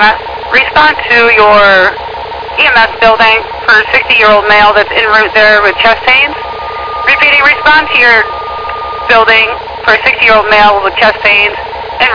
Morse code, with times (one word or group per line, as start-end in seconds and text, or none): none